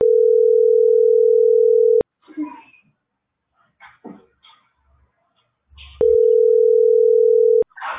Ringback tone (North America), 440 + 480 Hz.